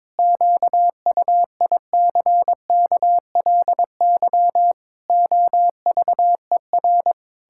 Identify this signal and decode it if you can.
Morse code, heard as a keyed tone, about 22 wpm, 700 Hz. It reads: QUICKLY OVER
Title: Pangram Challenge v2 - 22wpm